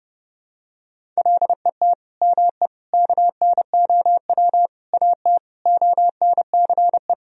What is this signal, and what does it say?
Morse code, heard as a keyed tone, about 32 wpm, 700 Hz. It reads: LET ME KNOW AT ONCE
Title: Sentences from Top 300 Words and Abbreviations 30wpm